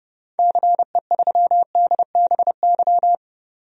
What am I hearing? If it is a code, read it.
Morse code, heard as a keyed tone, about 30 wpm, 700 Hz. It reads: CE3DBY
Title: DXCC-100 Call Signs - 6 Characters - 30wpm